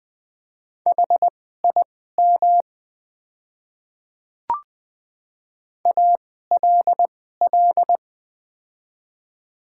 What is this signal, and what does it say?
Morse code, heard as a keyed tone, about 20 wpm, spaced as for 15 wpm, 690 Hz. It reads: HIM ALL
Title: 2x Character Spacing - Top 200 Words - 20wpm